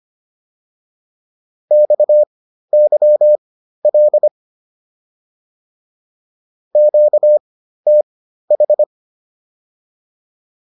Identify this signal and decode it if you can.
Morse code, heard as a keyed tone, about 25 wpm, spaced as for 13 wpm, 600 Hz. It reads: XYL QTH